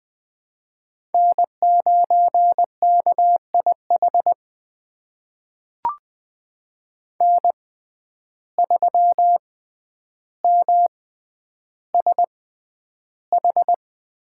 Morse code, heard as a keyed tone, about 20 wpm, spaced as for 10 wpm, 700 Hz. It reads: N9KIH N3MSH
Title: USA Call Signs - 5 Characters - 6x Character Spacing - 20wpm